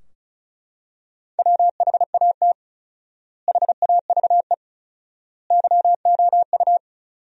Morse code, heard as a keyed tone, about 36 wpm, 700 Hz. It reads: WHAT HAVE YOU
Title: 4x Word-Spacing - Sentences from Top 400 Words - 35wpm